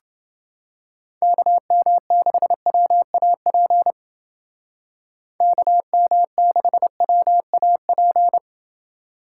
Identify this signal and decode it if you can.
Morse code, heard as a keyed tone, about 29 wpm, 700 Hz. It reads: KM6WAP KM6WAP